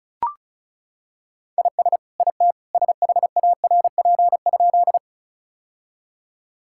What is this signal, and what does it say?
Morse code, heard as a keyed tone, about 34 wpm, 700 Hz. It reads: IS IT SHARP?
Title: Sentences from Top 1600 Words 35wpm